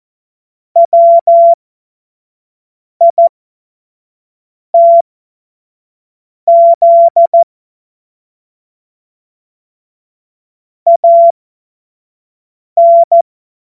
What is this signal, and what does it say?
Morse code, heard as a keyed tone, about 14 wpm, spaced as for 5 wpm, 680 Hz. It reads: WITZ AN
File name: Morsekurs / Witze / Witz 21. Februar 2023 25 F70